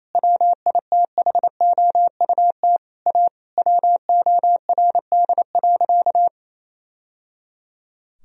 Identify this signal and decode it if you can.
Morse code, heard as a keyed tone, about 28 wpm, 700 Hz. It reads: WITHOUT A WORD.
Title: Sentences from Top 1000 Words - Review Part 3 - 28wpm